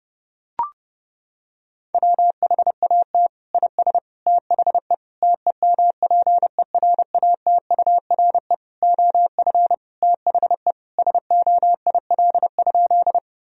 Morse code, heard as a keyed tone, about 32 wpm, 700 Hz. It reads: WHAT IS THE TEMPERATURE OF THE SOIL?
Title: Sentences from Top 1000 Words - Review Part 4 - 30wpm